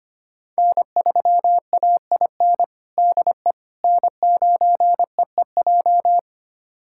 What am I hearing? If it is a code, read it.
Morse code, heard as a keyed tone, about 25 wpm, 700 Hz. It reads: N3AIN DE N9EEJ
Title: USA Callsign DE Callsign 25wpm